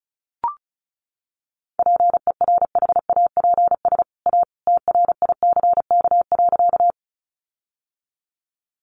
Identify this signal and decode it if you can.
Morse code, heard as a keyed tone, about 34 wpm, 700 Hz. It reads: PERHAPS A TRICK.
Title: Sentences from Top 1500 Words 35wpm